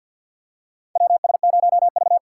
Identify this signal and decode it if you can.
Morse code, heard as a keyed tone, about 51 wpm, 700 Hz. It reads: WS0V
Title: USA Call Signs - 4 Characters - 6x Character Spacing - 50wpm